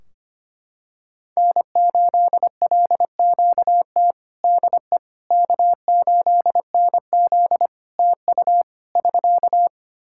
Morse code, heard as a keyed tone, about 25 wpm, 700 Hz. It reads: N8LQT DE K8NZ TU <SK>
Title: Practice QSO - Exchange 4 - Ending - 25wpm